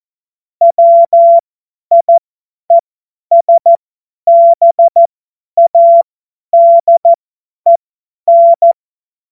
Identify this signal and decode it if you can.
Morse code, heard as a keyed tone, about 14 wpm, spaced as for 10 wpm, 680 Hz. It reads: WIESBADEN
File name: Morsekurs / Witze / Witz 03 Juni 2025 50 F70